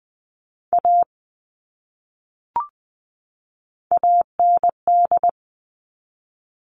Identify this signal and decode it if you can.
Morse code, heard as a keyed tone, about 20 wpm, 700 Hz. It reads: A AND
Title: Top 500 Words 20wpm